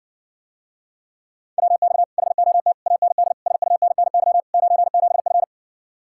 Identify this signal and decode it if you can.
Morse code, heard as a keyed tone, about 60 wpm, 700 Hz. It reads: WX HOT AND SUNNY 97F